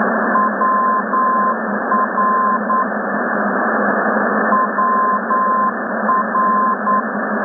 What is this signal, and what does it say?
Morse code, heard as a keyed tone, about 9 wpm, 1040 Hz, with noise. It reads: WR WR
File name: rsp2 ndb